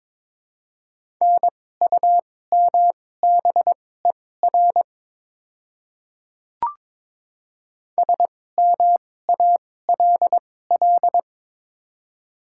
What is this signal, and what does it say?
Morse code, heard as a keyed tone, about 23 wpm, spaced as for 16 wpm, 700 Hz. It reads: NUMBER SMALL
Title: 2x Character Spacing - Top 200 Words - 22wpm